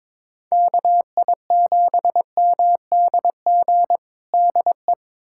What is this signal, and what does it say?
Morse code, heard as a keyed tone, about 22 wpm, 700 Hz. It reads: KI7MDG DE
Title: USA Callsign DE Callsign 22wpm